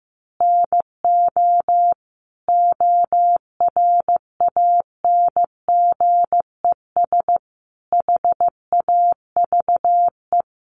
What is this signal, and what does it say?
Morse code, heard as a keyed tone, about 15 wpm, 700 Hz. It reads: NO ORANGES HAVE
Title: Easy Phrases 15wpm